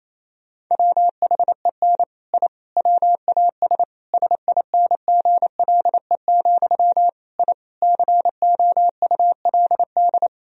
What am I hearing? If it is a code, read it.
Morse code, heard as a keyed tone, about 28 wpm, 700 Hz. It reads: WHEN I WAS SINGLE, I COULD
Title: Sentences from Top 1000 Words - Review Part 4 - 28wpm